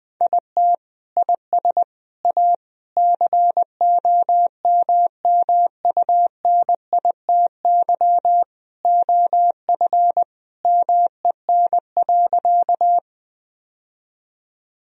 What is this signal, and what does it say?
Morse code, heard as a keyed tone, about 20 wpm, 700 Hz. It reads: IT IS A COMMUNITY OF MEN.
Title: Sentences from Top 1000 Words - Review Part 4 - 20wpm